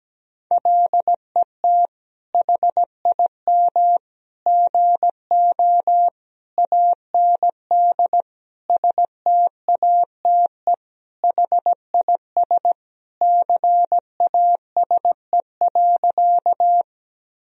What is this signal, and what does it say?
Morse code, heard as a keyed tone, about 17 wpm, 700 Hz. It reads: LET HIM GO AND STATE HIS CASE.
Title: Sentences from Top 1000 Words - Review Part 2 - 17wpm